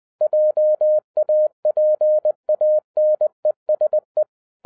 Morse code, heard as a keyed tone, about 20 wpm, 610 Hz. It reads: JAPANESE
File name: 2011-07-mp3 / n18 GREETINGS FROM SINGAPORE 20wpm